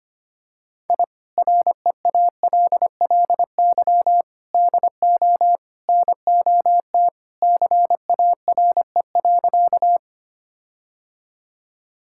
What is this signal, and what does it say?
Morse code, heard as a keyed tone, about 25 wpm, 700 Hz. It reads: I REALLY DO NOT CARE.
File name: practice / 25wpm / Sentences with 4-Word Phrases - 25wpm